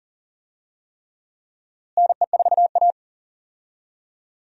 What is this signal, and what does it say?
Morse code, heard as a keyed tone, about 40 wpm, 700 Hz. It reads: NE4A
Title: USA Call Signs - 4 Characters - Repeated 3x - 40wpm